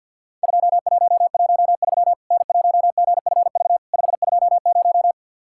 Morse code, heard as a keyed tone, about 50 wpm, 700 Hz. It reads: 2113 N1GFV 520
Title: Worked All Europe DX Contest - QTC Exchange - 50wpm